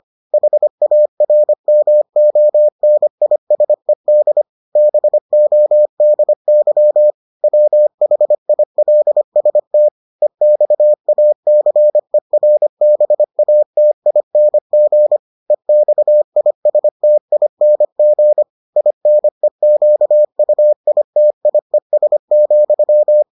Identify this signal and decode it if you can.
Morse code, heard as a keyed tone, about 25 wpm, 600 Hz. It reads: HARMONISED BODY WHILST EXACERBATING EXISTING INEQUITIES,